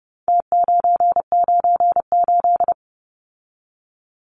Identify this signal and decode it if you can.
Morse code, heard as a keyed tone, about 30 wpm, 700 Hz. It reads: T998
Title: Numbers - Contest Style - 4-Digit 30wpm